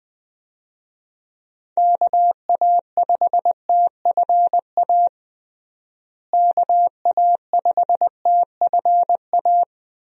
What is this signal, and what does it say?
Morse code, heard as a keyed tone, about 20 wpm, 700 Hz. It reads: KA5TFA KA5TFA
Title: USA Call Signs - 6 Characters - Repeated 2x - 20wpm